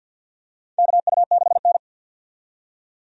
Morse code, heard as a keyed tone, about 52 wpm, 700 Hz. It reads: KU6N
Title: USA Call Signs - 4 Characters - 50wpm